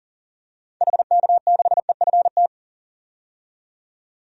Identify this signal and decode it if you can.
Morse code, heard as a keyed tone, about 40 wpm, 700 Hz. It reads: HK6EFT
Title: DXCC-100 Call Signs - 6 Characters - 40wpm